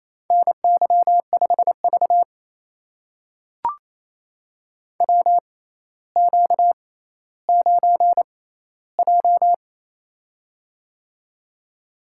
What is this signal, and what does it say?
Morse code, heard as a keyed tone, about 27 wpm, spaced as for 14 wpm, 700 Hz. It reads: NY5V WQ9J
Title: USA Call Signs - 4 Characters - 6x Character Spacing - 28wpm